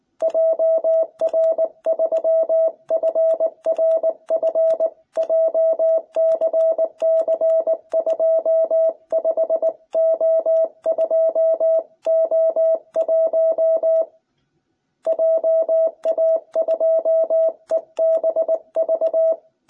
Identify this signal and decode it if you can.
Morse code, heard as a keyed tone, about 19 wpm, 660 Hz. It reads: JR3FRF JCC25O2O1 JA2EBV